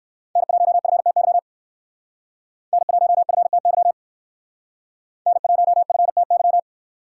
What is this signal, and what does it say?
Morse code, heard as a keyed tone, about 58 wpm, 700 Hz. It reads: N1FTY N1FTY N1FTY